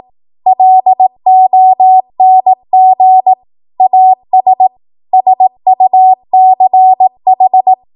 Morse code, heard as a keyed tone, about 18 wpm, 750 Hz. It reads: LONG AS SUCH
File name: Morse / 18WPM